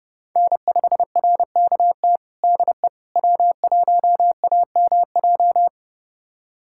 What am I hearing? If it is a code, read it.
Morse code, heard as a keyed tone, about 32 wpm, 700 Hz. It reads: N5RKT DE W1AMJ
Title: USA Callsign DE Callsign - 5 Characters - 30wpm